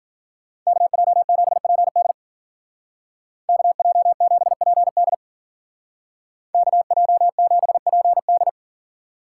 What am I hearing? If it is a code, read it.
Morse code, heard as a keyed tone, about 47 wpm, 700 Hz. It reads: KJ7PD KJ7PD KJ7PD
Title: Speed Racing - USA 2x2 Call Signs 40wpm